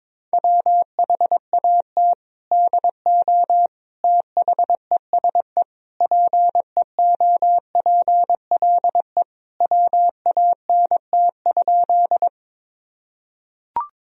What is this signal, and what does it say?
Morse code, heard as a keyed tone, about 22 wpm, 700 Hz. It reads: WHAT DO THESE PEOPLE WANT?
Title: Sentences from Top 1000 Words - Review Part 1 - 22wpm